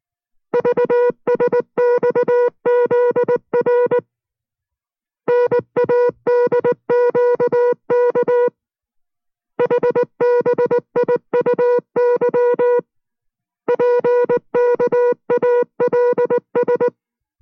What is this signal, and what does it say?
Morse code, heard as a keyed tone, about 19 wpm, 490 Hz. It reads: VSXZR NADQK HBIUY PKALS